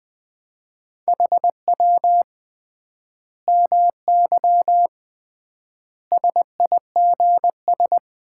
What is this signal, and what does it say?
Morse code, heard as a keyed tone, about 20 wpm, 700 Hz. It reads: HW MY SIGS
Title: Common QSO Phrases - 3x Word-Spacing - 20wpm